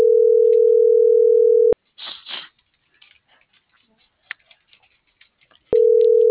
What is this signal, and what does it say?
Ringback tone (North America), 440 + 480 Hz.